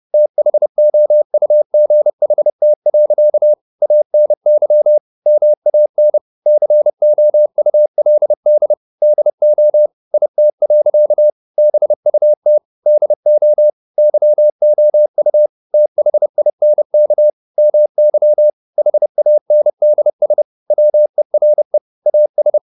Morse code, heard as a keyed tone, about 30 wpm, 600 Hz. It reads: THOUGHT. ANY MAN COULD DO IT. BUT DO YOU THINK MY HANDS WERE AS